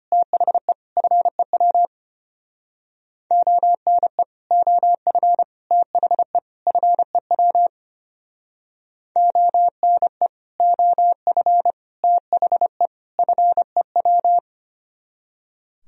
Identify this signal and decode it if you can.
Morse code, heard as a keyed tone, about 31 wpm, 700 Hz. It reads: THE FEW ONE OF THE FEW ONE OF THE FEW